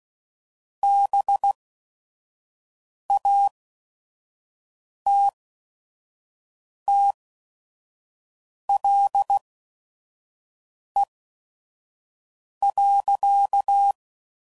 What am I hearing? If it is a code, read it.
Morse code, heard as a keyed tone, about 16 wpm, spaced as for 5 wpm, 800 Hz. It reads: BATTLE.